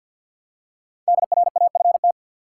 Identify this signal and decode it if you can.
Morse code, heard as a keyed tone, about 49 wpm, 700 Hz. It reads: DRAFT
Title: Five Letter Words - 50wpm